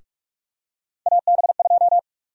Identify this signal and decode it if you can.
Morse code, heard as a keyed tone, about 46 wpm, 700 Hz. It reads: AB2
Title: USA Call Sign Prefixes - 45wpm